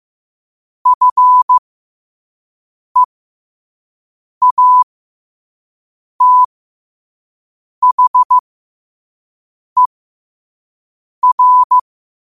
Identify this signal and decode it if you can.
Morse code, heard as a keyed tone, about 15 wpm, spaced as for 5 wpm, 1000 Hz. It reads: FEATHER
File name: qotdfiles / 5wpm